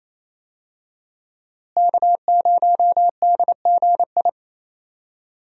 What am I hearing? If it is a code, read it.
Morse code, heard as a keyed tone, about 28 wpm, 700 Hz. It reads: K0DGI
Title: USA Call Signs - 5 Characters - Repeated 2x - 28wpm